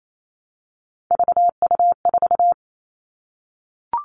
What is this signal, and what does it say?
Morse code, heard as a keyed tone, about 28 wpm, 700 Hz. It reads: VU4
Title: All Cut Numbers - 3-Digit 28wpm